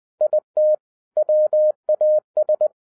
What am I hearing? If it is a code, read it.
Morse code, heard as a keyed tone, about 20 wpm, 610 Hz. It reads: IT WAS